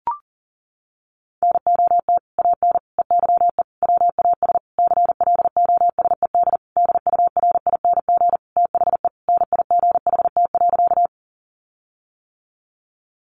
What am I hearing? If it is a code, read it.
Morse code, heard as a keyed tone, about 37 wpm, 700 Hz. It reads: NOT AN EYE WAS CLOSED DURING THE NIGHT.
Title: Sentences from Top 1400 Words 40wpm